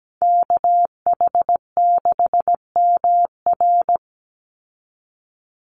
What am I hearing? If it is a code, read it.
Morse code, heard as a keyed tone, about 17 wpm, 700 Hz. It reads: KH6MR
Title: USA Call Signs 17wpm